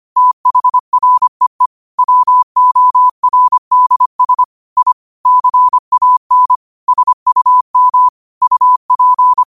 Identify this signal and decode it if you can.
Morse code, heard as a keyed tone, about 25 wpm, 1000 Hz. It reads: THREE WORDS I CAN SUM UP